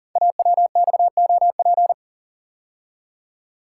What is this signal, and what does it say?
Morse code, heard as a keyed tone, about 38 wpm, 700 Hz. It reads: AWXOP
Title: Code Groups - 5 Letters - 40wpm